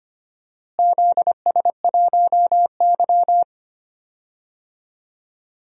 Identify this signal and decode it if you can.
Morse code, heard as a keyed tone, about 25 wpm, 700 Hz. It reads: ZS1Y
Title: DXCC-100 Call Signs - 4 Characters - Repeated 2x - 25wpm